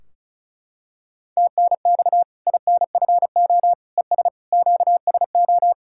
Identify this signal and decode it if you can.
Morse code, heard as a keyed tone, about 34 wpm, 700 Hz. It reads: TNX INFO ES QSO
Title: Practice QSO - Exchange 4 - Ending - 35wpm